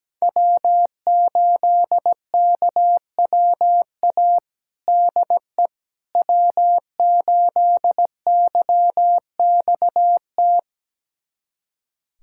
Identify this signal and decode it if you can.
Morse code, heard as a keyed tone, about 17 wpm, 700 Hz. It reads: W8KWA DE W8YXT
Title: USA Callsign DE Callsign - 5 Characters - 17wpm